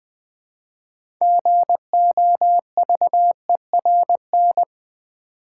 Morse code, heard as a keyed tone, about 20 wpm, 700 Hz. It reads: GOVERN
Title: Build A Word - 2 Letters At A Time - 6-Letter Words - 20wpm